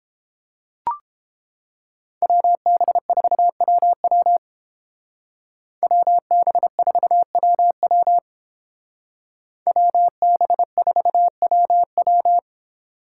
Morse code, heard as a keyed tone, about 31 wpm, 700 Hz. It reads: WB4WW WB4WW WB4WW